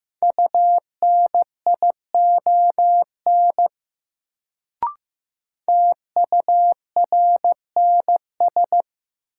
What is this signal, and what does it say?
Morse code, heard as a keyed tone, about 15 wpm, 700 Hz. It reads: UNION TURNS
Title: Sets of 5 Words Encore - New Word Drill - 15wpm